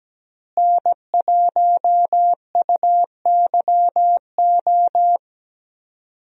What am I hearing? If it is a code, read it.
Morse code, heard as a keyed tone, about 17 wpm, 700 Hz. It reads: N1UYO
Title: USA Call Signs - 5 Characters - Repeated 3x - 17wpm